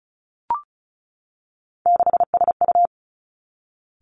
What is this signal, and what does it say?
Morse code, heard as a keyed tone, about 36 wpm, 700 Hz. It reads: 6SU
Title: Three Challenging Letter-Number Combinations 35wpm